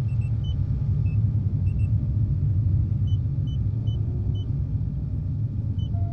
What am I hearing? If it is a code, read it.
There is no code.